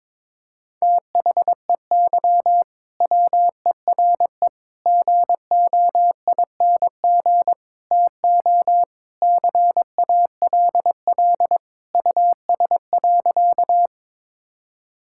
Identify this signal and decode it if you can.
Morse code, heard as a keyed tone, about 21 wpm, 700 Hz. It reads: THEY WERE GOING TO CALL US.
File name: practice / 22wpm / Sentences with 4-Word Phrases - 22wpm